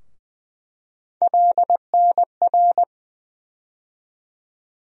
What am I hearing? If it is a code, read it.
Morse code, heard as a keyed tone, about 20 wpm, 700 Hz. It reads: LNR
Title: USA Call Sign Postfixes - 3 Letters 20wpm